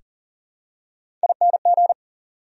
Morse code, heard as a keyed tone, about 40 wpm, 700 Hz. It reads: ING
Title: Prefixes and Suffixes 40wpm